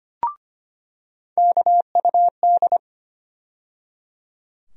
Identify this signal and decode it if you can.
Morse code, heard as a keyed tone, about 25 wpm, 700 Hz. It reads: KUD
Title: USA Call Sign Postfixes - 3 Letters 25wpm